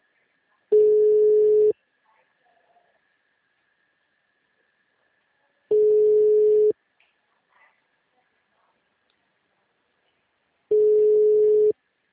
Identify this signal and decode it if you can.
Ringback tone (Europe, 425 Hz).
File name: out- -2008- - - .48134